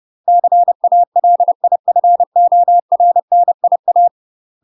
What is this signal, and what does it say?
Morse code, heard as a keyed tone, about 30 wpm, 700 Hz. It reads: CALIFORNIA
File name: UK MASTER QSOs TRAINING / 30WPM / UK MASTER QSOs 0004